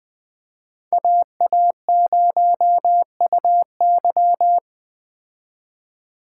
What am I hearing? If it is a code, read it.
Morse code, heard as a keyed tone, about 20 wpm, 700 Hz. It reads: AA0UY